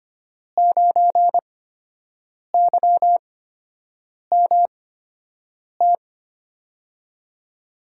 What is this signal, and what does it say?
Morse code, heard as a keyed tone, about 25 wpm, spaced as for 7 wpm, 700 Hz. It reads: 9YMT
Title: USA Call Signs - 5 Characters - 8x Character Spacing - 25wpm